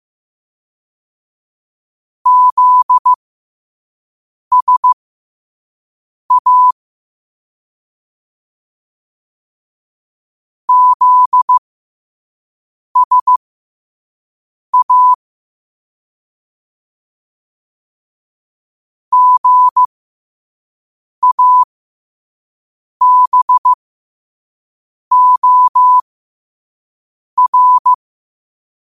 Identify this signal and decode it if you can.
Morse code, heard as a keyed tone, about 15 wpm, spaced as for 5 wpm, 1000 Hz. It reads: ZSA ZSA GABOR
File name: qotdfiles / 5wpm